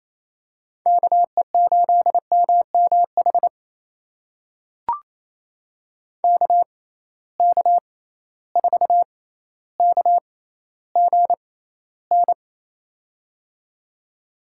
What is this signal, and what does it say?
Morse code, heard as a keyed tone, about 28 wpm, spaced as for 14 wpm, 700 Hz. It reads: KE8MMH KK4KGN
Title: USA Call Signs - 6 Characters - 6x Character Spacing - 28wpm